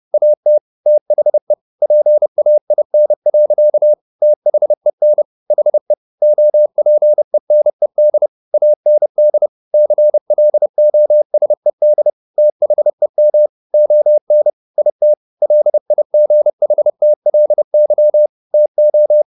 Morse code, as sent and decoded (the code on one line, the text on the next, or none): .- - / - .... / .--. .- .. -. .-.-.- / - .... -. / .... / --- .--. -. -.. / .- -. -.. / -.-. .-.. --- ... -.. / - .... -- / --- -. / .. - / .-.. .. --. .... - .-.. -.-- / - ---
AT THE PAIN. THEN HE OPENED AND CLOSED THEM ON IT LIGHTLY TO